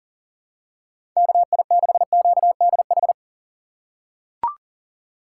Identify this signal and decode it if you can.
Morse code, heard as a keyed tone, about 40 wpm, 700 Hz. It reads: KI6QDH
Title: K1USN SST QSO Practice - 40wpm